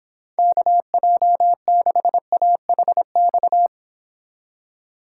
Morse code, heard as a keyed tone, about 24 wpm, 700 Hz. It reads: KJ6AHX